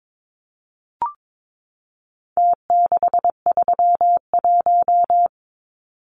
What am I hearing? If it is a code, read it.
Morse code, heard as a keyed tone, about 22 wpm, 700 Hz. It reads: T631
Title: Numbers - Contest Style - 4-Digit 22wpm